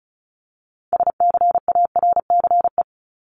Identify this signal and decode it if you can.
Morse code, heard as a keyed tone, about 34 wpm, 700 Hz. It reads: SCARCE
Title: LY Suffix - Top 100 Words with Base Words 35wpm